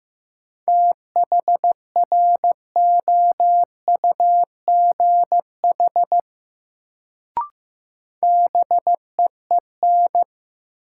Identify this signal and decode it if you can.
Morse code, heard as a keyed tone, about 15 wpm, 700 Hz. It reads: THROUGH BEEN
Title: Sets of 2 Words - New Word Drill - 15wpm